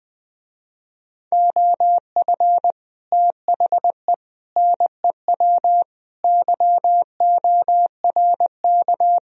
Morse code, heard as a keyed tone, about 20 wpm, 700 Hz. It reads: OF THE NEW YORK